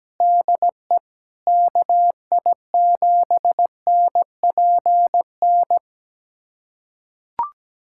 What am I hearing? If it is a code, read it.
Morse code, heard as a keyed tone, about 17 wpm, 700 Hz. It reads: DE KI7NPN
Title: USA Callsign DE Callsign 17wpm